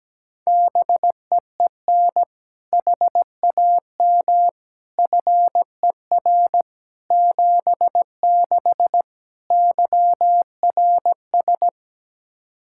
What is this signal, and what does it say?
Morse code, heard as a keyed tone, about 17 wpm, 700 Hz. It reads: BEEN HAM FER 76 YRS
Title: K1USN SST QSO Practice 17wpm